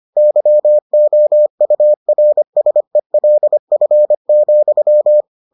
Morse code, heard as a keyed tone, about 25 wpm, 600 Hz. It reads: YOURSELF,